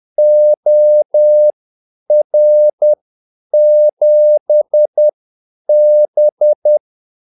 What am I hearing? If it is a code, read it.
Morse code, heard as a keyed tone, about 10 wpm, spaced as for 8 wpm, 600 Hz. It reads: OR7B
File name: Callsigns-10-8wpm